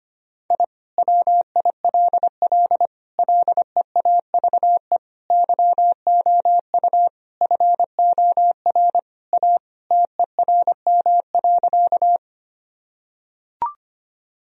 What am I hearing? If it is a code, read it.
Morse code, heard as a keyed tone, about 25 wpm, 700 Hz. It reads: I WILL LEAVE YOU FOR A TERM.
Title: Sentences from Top 1000 Words - Review Part 4 - 25wpm